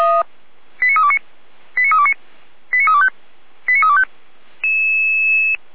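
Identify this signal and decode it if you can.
Five-tone selective call (ZVEI), heard as quick runs of five tones, tones of about 80 ms. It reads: 89329 89329 89325 89325
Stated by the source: ZVEI five-tone selective call 89250, received by SDR and served by Feuerwehr Orlamünde.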